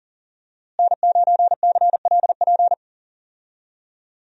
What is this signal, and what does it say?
Morse code, heard as a keyed tone, about 40 wpm, 700 Hz. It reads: N9CLP
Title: USA Call Signs - 5 Characters - Repeated 2x - 40wpm